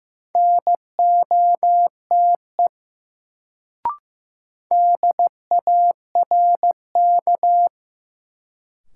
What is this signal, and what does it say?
Morse code, heard as a keyed tone, about 16 wpm, 700 Hz. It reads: NOTE DARK
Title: Sentences from Top 500 Words - New Word Drill - 15wpm